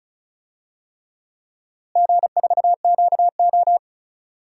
Morse code, heard as a keyed tone, about 34 wpm, 700 Hz. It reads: G4QO
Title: DXCC-100 Call Signs - 4 Characters - Repeated 3x - 35wpm